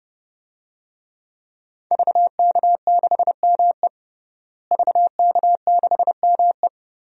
Morse code, heard as a keyed tone, about 32 wpm, 700 Hz. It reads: VK6ME VK6ME
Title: DXCC-100 Call Signs - 5 Characters - Repeated 2x - 30wpm